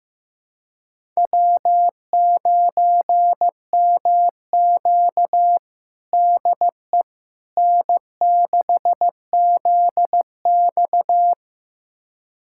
Morse code, heard as a keyed tone, about 15 wpm, 700 Hz. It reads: W9MQ DE N6ZX
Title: USA Callsign DE Callsign - 4 Characters - 15wpm